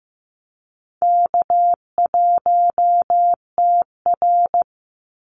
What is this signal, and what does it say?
Morse code, heard as a keyed tone, about 15 wpm, 700 Hz. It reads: K1TR
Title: Easy Call Signs Repeated 3x - 15wpm